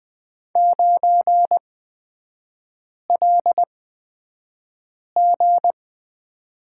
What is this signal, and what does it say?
Morse code, heard as a keyed tone, about 20 wpm, spaced as for 5 wpm, 700 Hz. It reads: 9LG
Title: 255 Lesson 41e - Period - Three Characters 20wpm